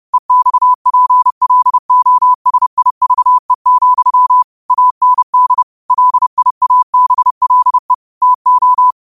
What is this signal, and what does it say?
Morse code, heard as a keyed tone, about 30 wpm, 1000 Hz. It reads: EXPLOSIVE, AND LIABLE TO
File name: qotdfiles / 30wpm